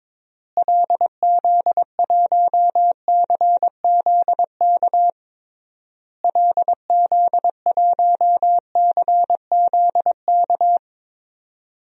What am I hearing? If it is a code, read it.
Morse code, heard as a keyed tone, about 21 wpm, 700 Hz. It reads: LZ1CZK LZ1CZK
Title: DXCC-100 Call Signs - 6 Characters - Repeated 3x - 22wpm